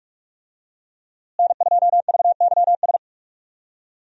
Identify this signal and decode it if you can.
Morse code, heard as a keyed tone, about 46 wpm, 700 Hz. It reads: N2VYS
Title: USA Call Signs - 5 Characters - Repeated 2x - 45wpm